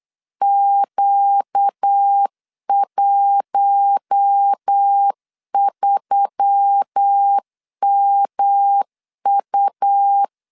Morse code, heard as a keyed tone, about 8 wpm, 800 Hz. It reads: Q13MU
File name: LEZIONI / 6 PKX0 / 6 PK 8wpm 3003